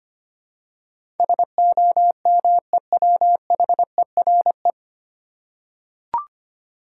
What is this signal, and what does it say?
Morse code, heard as a keyed tone, about 25 wpm, 700 Hz. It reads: SOMEWHERE
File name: Nine Letter Words - 25wpm